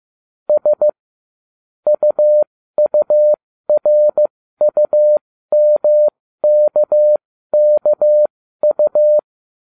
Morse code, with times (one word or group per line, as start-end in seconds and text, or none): 0.5-0.9 S
1.9-9.2 UURUMKKU